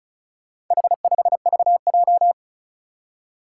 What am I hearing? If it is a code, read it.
Morse code, heard as a keyed tone, about 36 wpm, 700 Hz. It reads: H5VJ
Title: Four Challenging Letter-Number Combinations 35wpm